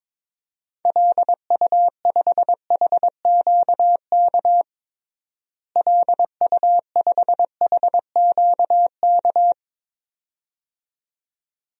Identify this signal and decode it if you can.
Morse code, heard as a keyed tone, about 22 wpm, 700 Hz. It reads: LU5HQK LU5HQK
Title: DXCC-100 Call Signs - 6 Characters - Repeated 2x - 22wpm